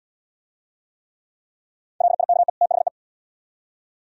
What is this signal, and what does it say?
Morse code, heard as a keyed tone, about 71 wpm, 700 Hz. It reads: LIKE THE